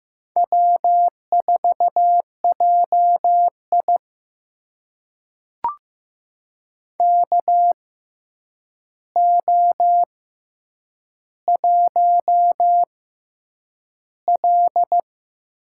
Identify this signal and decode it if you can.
Morse code, heard as a keyed tone, about 15 wpm, spaced as for 8 wpm, 700 Hz. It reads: W4JI KO1L
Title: USA Call Signs - 4 Characters - 6x Character Spacing - 15wpm